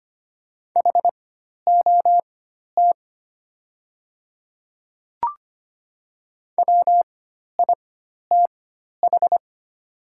Morse code, heard as a keyed tone, about 25 wpm, spaced as for 12 wpm, 700 Hz. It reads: HOT WITH
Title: 4x Character Spacing - Top 500 Words - 25wpm